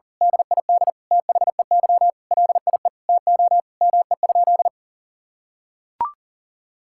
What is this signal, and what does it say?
Morse code, heard as a keyed tone, about 40 wpm, 700 Hz. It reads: DID THEY LIE TO ME?
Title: Sentences from Top 1600 Words 40wpm